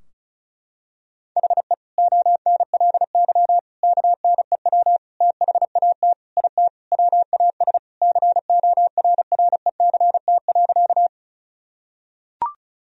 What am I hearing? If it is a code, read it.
Morse code, heard as a keyed tone, about 36 wpm, 700 Hz. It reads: HE ONLY KNEW THAT IT WAS CORRECT.
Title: Sentences from Top 1000 Words - Review Part 2 - 35wpm